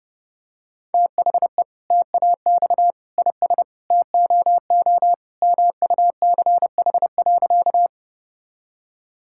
Morse code, heard as a keyed tone, about 32 wpm, 700 Hz. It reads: THE TAX IS TOO MUCH.
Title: Sentences from Top 1000 Words - Review Part 5 - 30wpm